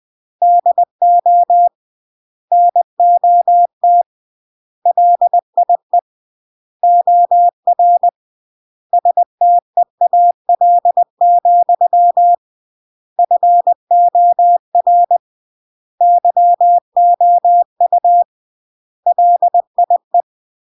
Morse code, heard as a keyed tone, about 20 wpm, 700 Hz. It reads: DO NOT LIE OR STEAL, FOR YOU LIE